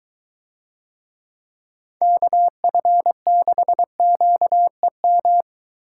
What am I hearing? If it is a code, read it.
Morse code, heard as a keyed tone, about 23 wpm, 700 Hz. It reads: KF6QEM